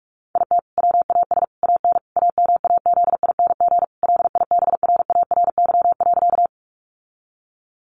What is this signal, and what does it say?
Morse code, heard as a keyed tone, about 44 wpm, 700 Hz. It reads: IT WAS AN AMAZING LIBRARY.
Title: Sentences from Top 1500 Words 45wpm